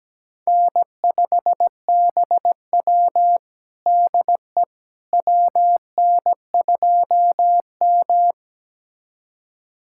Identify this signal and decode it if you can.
Morse code, heard as a keyed tone, about 17 wpm, 700 Hz. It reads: N5BW DE WN2M
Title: USA Callsign DE Callsign - 4 Characters - 17wpm